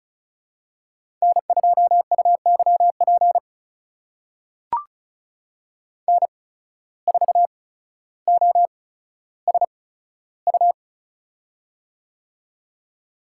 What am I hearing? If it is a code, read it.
Morse code, heard as a keyed tone, about 36 wpm, spaced as for 15 wpm, 700 Hz. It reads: N2UYP N4OSU
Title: USA Call Signs - 5 Characters - 8x Character Spacing - 35wpm